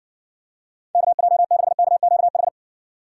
Morse code, heard as a keyed tone, about 60 wpm, 700 Hz. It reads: KJ6KQH